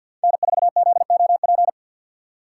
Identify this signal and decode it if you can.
Morse code, heard as a keyed tone, about 53 wpm, 700 Hz. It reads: N4ZOP
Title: USA Call Signs - 5 Characters - 4x Character Spacing - 50wpm